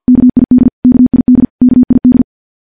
Morse code, heard as a keyed tone, about 50 wpm, 250 Hz. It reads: KID KID KID
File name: morse / morse-training-50WPM